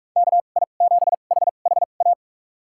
Morse code, heard as a keyed tone, about 43 wpm, spaced as for 32 wpm, 700 Hz. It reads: KI7HHA
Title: USA Call Signs - 2x Character Spacing - 45wpm